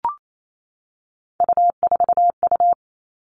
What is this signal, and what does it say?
Morse code, heard as a keyed tone, about 28 wpm, 700 Hz. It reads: U4U
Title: Three Challenging Letter-Number Combinations 28wpm